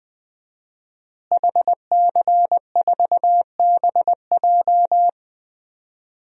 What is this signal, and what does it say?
Morse code, heard as a keyed tone, about 21 wpm, 700 Hz. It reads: HC4BJ
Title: DXCC-100 Call Signs - 5 Characters - Repeated 2x - 20wpm